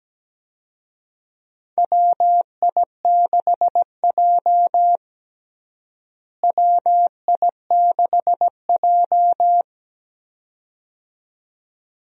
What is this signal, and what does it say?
Morse code, heard as a keyed tone, about 17 wpm, 700 Hz. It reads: WI6J WI6J
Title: USA Call Signs - 4 Characters - Repeated 2x - 17wpm